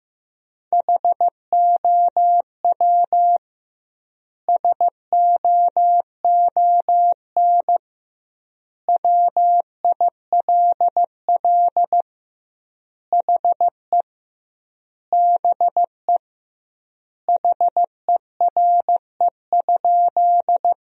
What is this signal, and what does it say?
Morse code, heard as a keyed tone, about 15 wpm, 700 Hz. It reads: HOW SOON WILL HE BE HERE?
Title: 2x Word-Spacing - Sentences from Top 400 Words - 15wpm